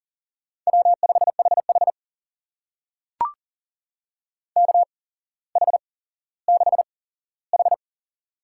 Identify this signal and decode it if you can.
Morse code, heard as a keyed tone, about 40 wpm, spaced as for 17 wpm, 700 Hz. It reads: W5HH KH6H
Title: USA Call Signs - 4 Characters - 8x Character Spacing - 40wpm